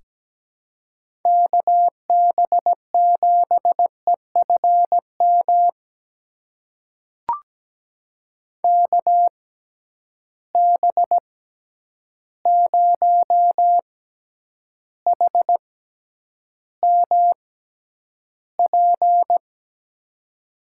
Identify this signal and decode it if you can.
Morse code, heard as a keyed tone, about 17 wpm, spaced as for 9 wpm, 700 Hz. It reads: KB7EFM KB0HMP
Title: USA Call Signs - 6 Characters - 6x Character Spacing - 17wpm